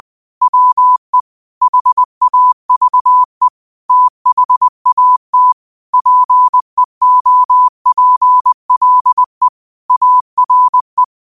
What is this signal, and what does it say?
Morse code, heard as a keyed tone, about 20 wpm, 1000 Hz. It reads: WE HAVE THAT PEOPLE ARE
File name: qotdfiles / 20wpm